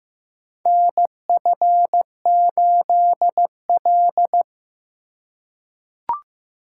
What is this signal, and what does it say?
Morse code, heard as a keyed tone, about 15 wpm, 700 Hz. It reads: NF8L
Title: USA Call Signs - 4 Characters - 2x Character Spacing - 15wpm